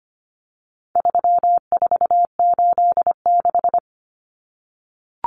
Morse code, heard as a keyed tone, about 25 wpm, 700 Hz. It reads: 3486
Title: Common Cut Numbers - 4-Digit 25wpm